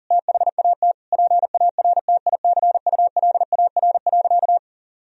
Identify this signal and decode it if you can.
Morse code, heard as a keyed tone, about 40 wpm, 700 Hz. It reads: THAT PARTICULAR.
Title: Sentences from Top 1000 Words - Review Part 4 - 40wpm